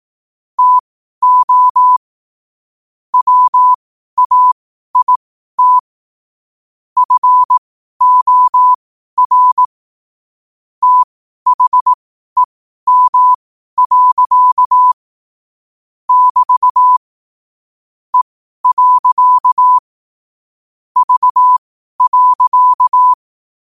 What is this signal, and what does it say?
Morse code, heard as a keyed tone, about 18 wpm, spaced as for 12 wpm, 1000 Hz. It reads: TO WAIT FOR THEM. <BT> E. V.
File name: 12wpm